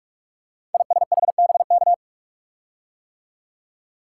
Morse code, heard as a keyed tone, about 44 wpm, 700 Hz. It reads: ISHBK
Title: Code Groups - 5 Letters - 45wpm